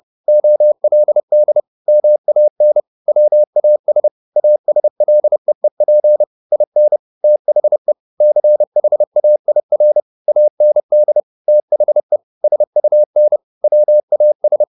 Morse code, heard as a keyed tone, about 30 wpm, 600 Hz. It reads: OLD MAN WAS ASLEEP IN THE CHAIR AND THE SUN WAS
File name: the old man and the sea / the old man and the sea 1 30wpm